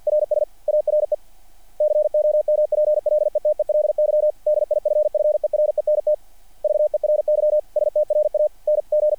Morse code, heard as a keyed tone, about 48 wpm, 660 Hz. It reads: RS ARE COMPLETELY DIFFERENT VERY STRANG
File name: archived / CRSnet